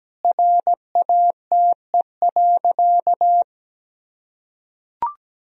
Morse code, heard as a keyed tone, about 17 wpm, 700 Hz. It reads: RATE.
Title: Sentences from Top 1000 Words - Review Part 4 - 17wpm